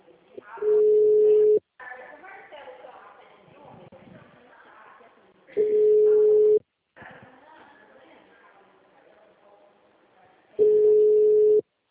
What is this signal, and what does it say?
Ringback tone (Europe, 425 Hz), with speech.
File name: out- -2024